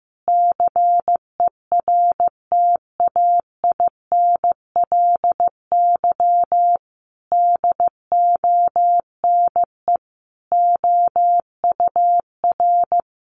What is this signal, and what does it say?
Morse code, heard as a keyed tone, about 15 wpm, 700 Hz. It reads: CERTAINLY DONE OUR
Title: Sentences from Top 900 Words 15wpm